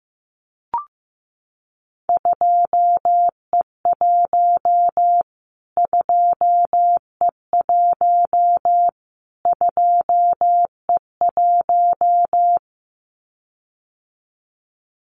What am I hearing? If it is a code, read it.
Morse code, heard as a keyed tone, about 15 wpm, 700 Hz. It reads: 2E1 2E1 2E1
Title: Callsign Prefixes for Top 25 DXCC - Repeated 3x - 15wpm